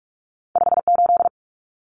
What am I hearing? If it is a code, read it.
Morse code, heard as a keyed tone, about 46 wpm, 700 Hz. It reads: 58
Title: Common Cut Numbers - 2-Digit 45wpm